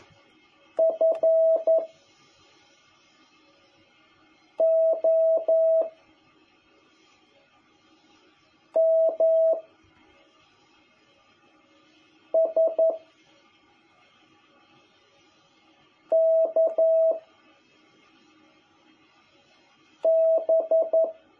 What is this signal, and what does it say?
Morse code, heard as a keyed tone, about 11 wpm, 660 Hz. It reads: F O M S K B
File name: jh8cbh / 2021cw